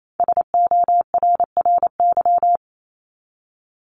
Morse code, heard as a keyed tone, about 28 wpm, 700 Hz. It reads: SORRY
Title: Top 1751-2000 Words 28wpm